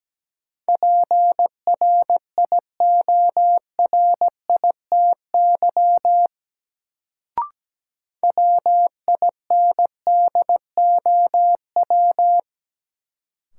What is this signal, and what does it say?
Morse code, heard as a keyed tone, about 17 wpm, 700 Hz. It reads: PRIORITY WINDOW
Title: Sentences from Top 1600 Words - New Word Drill - 17wpm